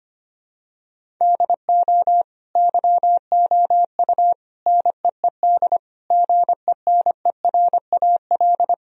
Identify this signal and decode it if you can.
Morse code, heard as a keyed tone, about 25 wpm, 700 Hz. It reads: DO YOU NEED GENERAL
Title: Sentences from Top 1000 Words - Review Part 3 - 25wpm